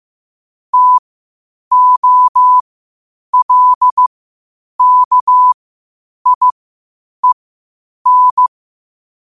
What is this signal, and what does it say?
Morse code, heard as a keyed tone, about 15 wpm, spaced as for 8 wpm, 1000 Hz. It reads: TOLKIEN
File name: qotdfiles / 8wpm